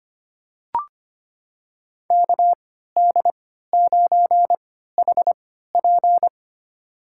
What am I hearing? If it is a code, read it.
Morse code, heard as a keyed tone, about 25 wpm, spaced as for 14 wpm, 700 Hz. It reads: KD9HP
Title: USA Call Signs - 5 Characters - 3x Character Spacing - 25wpm